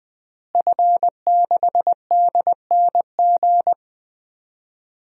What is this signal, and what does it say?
Morse code, heard as a keyed tone, about 21 wpm, 700 Hz. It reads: F6DNG
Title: DXCC-100 Call Signs - 5 Characters - Repeated 3x - 20wpm